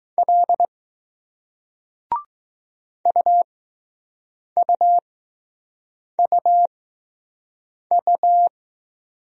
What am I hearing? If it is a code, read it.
Morse code, heard as a keyed tone, about 20 wpm, spaced as for 6 wpm, 700 Hz. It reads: L UUUU